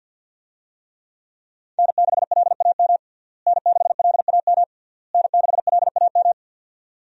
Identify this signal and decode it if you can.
Morse code, heard as a keyed tone, about 49 wpm, 700 Hz. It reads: N6LAM N6LAM N6LAM
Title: USA Call Signs - 5 Characters - Repeated 3x - 50wpm